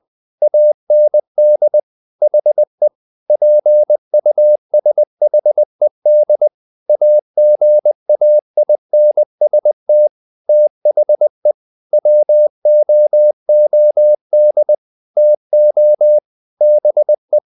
Morse code, heard as a keyed tone, about 20 wpm, 600 Hz. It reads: AND HE PUSHED AGAINST THE WOOD TO BE
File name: the old man and the sea / the old man and the sea part 1 20wpm